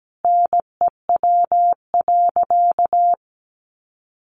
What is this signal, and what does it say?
Morse code, heard as a keyed tone, about 17 wpm, 700 Hz. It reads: NEW.
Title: Sentences from Top 1500 Words 17wpm